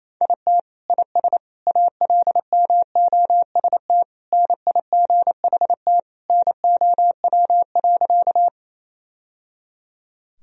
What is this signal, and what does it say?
Morse code, heard as a keyed tone, about 28 wpm, 700 Hz. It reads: IT IS ALMOST NIGHT NOW.
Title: Sentences from Top 1000 Words - Review Part 4 - 28wpm